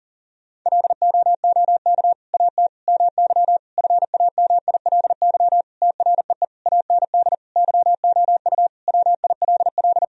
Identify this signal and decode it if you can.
Morse code, heard as a keyed tone, about 40 wpm, 700 Hz. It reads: LOOK AT MY FAMILY TREE AND YOU WILL
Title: Sentences from Top 1000 Words - Review Part 1 - 40wpm